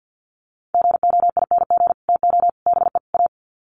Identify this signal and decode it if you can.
Morse code, heard as a keyed tone, about 50 wpm, 700 Hz. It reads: GOING TO BE A
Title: Sets of 4 Words 50wpm